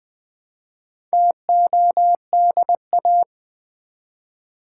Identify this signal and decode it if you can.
Morse code, heard as a keyed tone, about 20 wpm, 690 Hz. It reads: TODA
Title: Build A Word - 1 Letter At A Time - 5-Letter Words - 20wpm